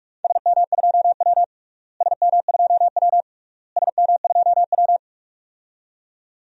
Morse code, heard as a keyed tone, about 43 wpm, 700 Hz. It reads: SM2W SM2W SM2W
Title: DXCC-100 Call Signs - 4 Characters - Repeated 3x - 45wpm